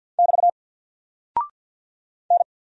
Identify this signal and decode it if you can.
Morse code, heard as a keyed tone, about 50 wpm, 700 Hz. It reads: <BT> N
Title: Single Character (Letter:Number:Punctuation:Prosign) 50wpm